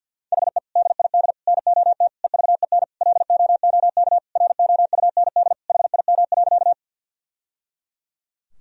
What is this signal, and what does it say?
Morse code, heard as a keyed tone, about 52 wpm, 700 Hz. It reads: HE DID NOT EVEN LOOK ROUND HIM.
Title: Sentences from Top 1000 Words - Review Part 1 - 50wpm